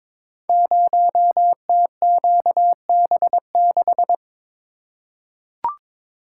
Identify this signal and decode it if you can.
Morse code, heard as a keyed tone, about 23 wpm, 700 Hz. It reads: 0TQB6
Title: Code Groups - 5 Letter:Numbers - 22wpm